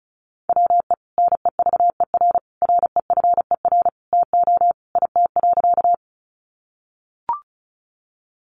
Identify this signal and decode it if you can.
Morse code, heard as a keyed tone, about 34 wpm, 700 Hz. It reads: WE NEVER REFER TO IT.
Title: Sentences from Top 1000 Words 35wpm